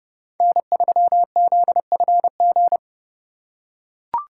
Morse code, heard as a keyed tone, about 30 wpm, 700 Hz. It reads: N3ZFG